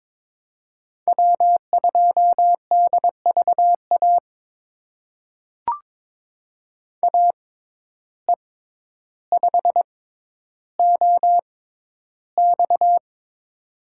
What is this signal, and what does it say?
Morse code, heard as a keyed tone, about 22 wpm, spaced as for 11 wpm, 700 Hz. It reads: W2DVA AE5OX
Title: USA Call Signs - 5 Characters - 6x Character Spacing - 22wpm